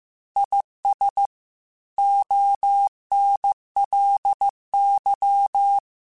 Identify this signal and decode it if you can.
Morse code, heard as a keyed tone, about 15 wpm, 800 Hz. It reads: IS ONLY
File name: Technical Subjects / Technical Subjects-mp3s / AND Gate @13